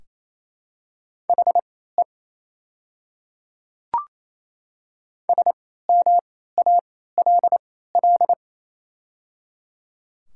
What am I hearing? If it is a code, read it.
Morse code, heard as a keyed tone, about 28 wpm, spaced as for 16 wpm, 700 Hz. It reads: HE SMALL
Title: 3x Character Spacing - Top 200 Words - 28wpm